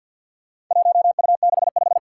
Morse code, heard as a keyed tone, about 49 wpm, 700 Hz. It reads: 1U65
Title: Four Challenging Letter-Number Combinations 50wpm